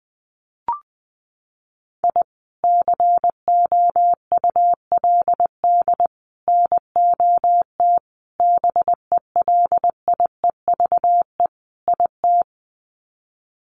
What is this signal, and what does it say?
Morse code, heard as a keyed tone, about 20 wpm, 700 Hz. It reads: I COULD NOT BELIEVE IT
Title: Sets of 5 Words Encore 20wpm